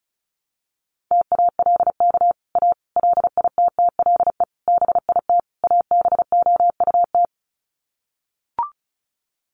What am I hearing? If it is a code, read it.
Morse code, heard as a keyed tone, about 34 wpm, 700 Hz. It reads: TALK A LITTLE BIT ABOUT
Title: Sets of 5 Words Encore 35wpm